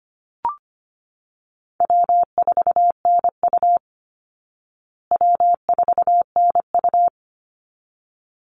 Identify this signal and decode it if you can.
Morse code, heard as a keyed tone, about 25 wpm, 700 Hz. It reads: W4NU W4NU
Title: Easy Call Signs Repeated 2x - 25wpm